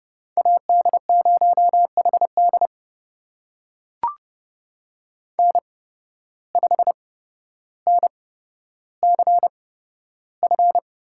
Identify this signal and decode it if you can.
Morse code, heard as a keyed tone, about 31 wpm, spaced as for 13 wpm, 700 Hz. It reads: AD0HD N5NCF
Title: USA Call Signs - 8x Character Spacing - 30wpm